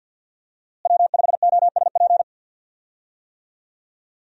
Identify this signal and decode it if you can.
Morse code, heard as a keyed tone, about 52 wpm, 690 Hz. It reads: W5OSP
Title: USA Call Signs - 5 Characters - Repeated 3x - 50wpm